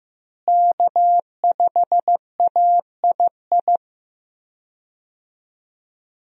Morse code, heard as a keyed tone, about 15 wpm, 700 Hz. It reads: K5AII
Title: USA Call Signs - 5 Characters - Repeated 2x - 15wpm